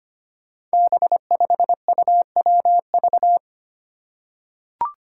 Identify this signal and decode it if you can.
Morse code, heard as a keyed tone, about 25 wpm, 700 Hz. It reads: B5UWV
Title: Five Challenging Letter-Number Combinations 25wpm